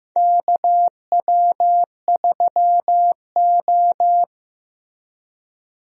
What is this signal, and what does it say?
Morse code, heard as a keyed tone, about 15 wpm, 700 Hz. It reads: KW3O
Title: USA Call Signs - 4 Characters - Repeated 3x - 15wpm